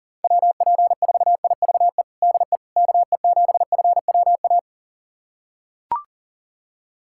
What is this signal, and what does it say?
Morse code, heard as a keyed tone, about 40 wpm, 700 Hz. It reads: WP4IVE DE KE7FWA
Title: USA Callsign DE Callsign 40wpm